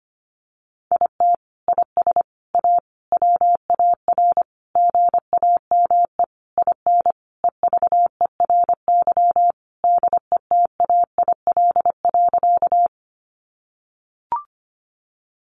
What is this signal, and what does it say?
Morse code, heard as a keyed tone, about 25 wpm, 700 Hz. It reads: IT IS A WAR GAME IN EVERY DETAIL.
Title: Sentences from Top 1500 Words 25wpm